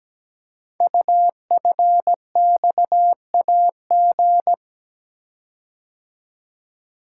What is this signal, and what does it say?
Morse code, heard as a keyed tone, about 17 wpm, 700 Hz. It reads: UFXAG
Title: Code Groups - 5 Letters - 17wpm